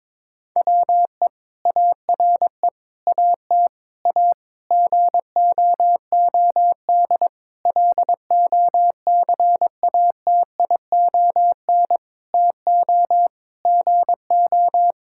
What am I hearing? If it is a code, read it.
Morse code, heard as a keyed tone, about 21 wpm, 700 Hz. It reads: WE ARE AT A GOOD LOCATION TO GO
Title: Sentences from Top 1000 Words - Review Part 6 - 22wpm